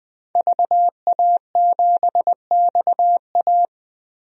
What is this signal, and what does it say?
Morse code, heard as a keyed tone, about 20 wpm, 700 Hz. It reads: VA7XA
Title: DXCC-100 Call Signs - 5 Characters - Repeated 3x - 20wpm